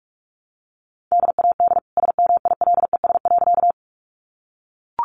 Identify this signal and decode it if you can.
Morse code, heard as a keyed tone, about 44 wpm, 700 Hz. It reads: DAD SMILES.
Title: Easy Phrases 45wpm